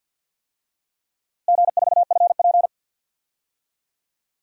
Morse code, heard as a keyed tone, about 51 wpm, 700 Hz. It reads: G4FP
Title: DXCC-100 Call Signs - 4 Characters - Repeated 3x - 50wpm